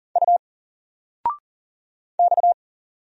Morse code, heard as a keyed tone, about 40 wpm, 700 Hz. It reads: U X
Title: Single Letter-Number 40wpm